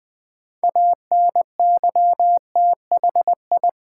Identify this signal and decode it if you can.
Morse code, heard as a keyed tone, about 20 wpm, 700 Hz. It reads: ANYTHI
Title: Build A Word - 3 Letters At A Time - 8-Letter Words - 20wpm